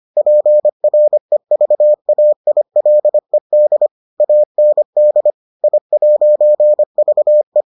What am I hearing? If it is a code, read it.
Morse code, heard as a keyed tone, about 25 wpm, 600 Hz. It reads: PREVAILED AND I'VE